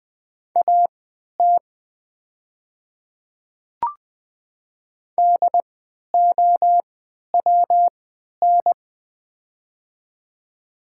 Morse code, heard as a keyed tone, about 21 wpm, spaced as for 11 wpm, 700 Hz. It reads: AT DOWN
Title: 3x Character Spacing - Top 100 Words - 20wpm